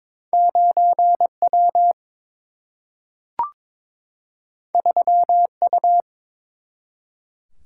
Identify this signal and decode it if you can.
Morse code, heard as a keyed tone, about 22 wpm, 700 Hz. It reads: 9W 3U
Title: Single-Letter Single-Number 22wpm